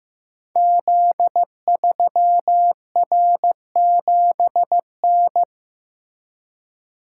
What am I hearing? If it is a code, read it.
Morse code, heard as a keyed tone, about 15 wpm, 700 Hz. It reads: Z3R7N
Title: Code Groups - 5 Letter:Numbers - 15wpm